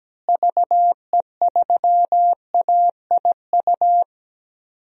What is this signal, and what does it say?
Morse code, heard as a keyed tone, about 17 wpm, 700 Hz. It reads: VE3AIU
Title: DXCC-100 Call Signs - 6 Characters - Repeated 2x - 17wpm